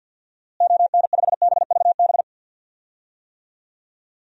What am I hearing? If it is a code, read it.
Morse code, heard as a keyed tone, about 49 wpm, 700 Hz. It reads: ON5BVB
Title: DXCC-100 Call Signs - 6 Characters - Repeated 3x - 50wpm